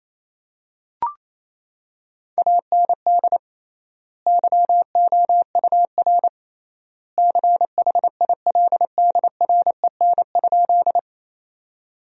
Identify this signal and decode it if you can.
Morse code, heard as a keyed tone, about 28 wpm, 700 Hz. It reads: AND YOUR CHILDREN?
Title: 3x Word-Spacing - Sentences from Top 300 Words - 28wpm